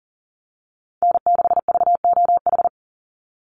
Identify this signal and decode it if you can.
Morse code, heard as a keyed tone, about 40 wpm, 700 Hz. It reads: N6VOH
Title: Call Signs Repeated 3x - 40wpm